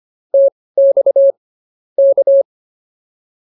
Morse code, heard as a keyed tone, about 24 wpm, spaced as for 18 wpm, 550 Hz. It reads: TX K
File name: POTA108 18